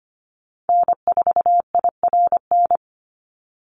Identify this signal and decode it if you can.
Morse code, heard as a keyed tone, about 25 wpm, 700 Hz. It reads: N4IRN
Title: Call Signs Repeated 3x - 25wpm